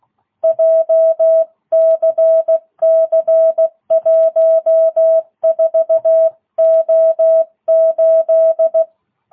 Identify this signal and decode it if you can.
Morse code, heard as a keyed tone, about 16 wpm, 660 Hz. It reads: JCC14O8
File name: jh8cbh / 2022cw